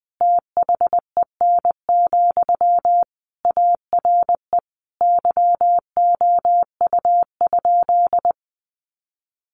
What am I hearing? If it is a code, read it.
Morse code, heard as a keyed tone, about 20 wpm, 700 Hz. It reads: THEN, ARE YOU?
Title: Sentences from Top 100 Words 20wpm